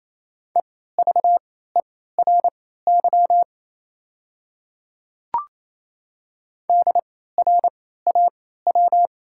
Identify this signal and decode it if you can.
Morse code, heard as a keyed tone, about 28 wpm, spaced as for 16 wpm, 690 Hz. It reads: EVERY DRAW
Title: 3x Character Spacing - Top 300 Words - 28wpm